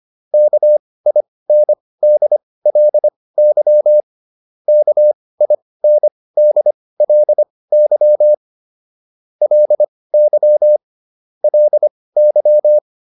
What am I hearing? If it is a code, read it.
Morse code, heard as a keyed tone, about 25 wpm, spaced as for 18 wpm, 600 Hz. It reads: KINDLY KINDLY LY LY
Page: intermediate-practice-files